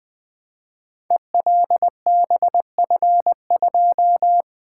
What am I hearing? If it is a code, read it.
Morse code, heard as a keyed tone, about 20 wpm, 700 Hz. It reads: ELBF2
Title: Code Groups - 5 Letter:Numbers - 20wpm